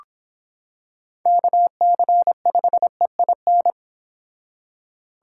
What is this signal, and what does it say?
Morse code, heard as a keyed tone, about 25 wpm, 700 Hz. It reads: KC5EIN